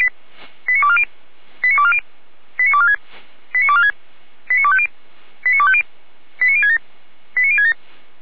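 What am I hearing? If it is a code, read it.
Five-tone selective call (ZVEI), heard as quick runs of five tones, tones of about 70 ms. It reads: 89240 89240 89246 89246 89250 89250 89067 89067
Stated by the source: ZVEI five-tone selective call 89160 received on SDR, from the Feuerwehr Orlamünde fire brigade audio server.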